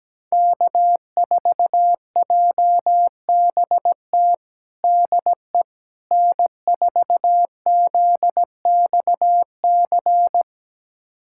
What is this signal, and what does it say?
Morse code, heard as a keyed tone, about 17 wpm, 700 Hz. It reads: K4JBT DE N4ZXC
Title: USA Callsign DE Callsign - 5 Characters - 17wpm